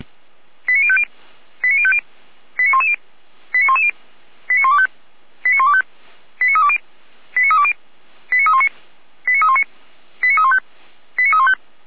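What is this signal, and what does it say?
Five-tone selective call (ZVEI), heard as quick runs of five tones, tones of about 70 ms. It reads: 89050 89050 89110 89110 89125 89125 89320 89320 89319 89319 89315 89315
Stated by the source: ZVEI five-tone call received with an SDR, recorded for Feuerwehr Orlamünde.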